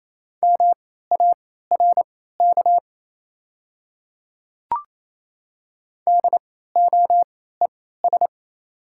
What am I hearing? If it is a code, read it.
Morse code, heard as a keyed tone, about 28 wpm, spaced as for 16 wpm, 700 Hz. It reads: MARK DOES
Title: 3x Character Spacing - Top 300 Words - 28wpm